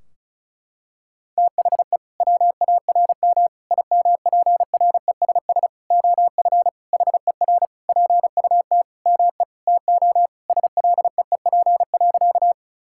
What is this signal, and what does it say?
Morse code, heard as a keyed tone, about 36 wpm, 700 Hz. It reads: THE WARM IMPRESS OF HER PUT ME TO SLEEP.
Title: Sentences from Top 1600 Words 35wpm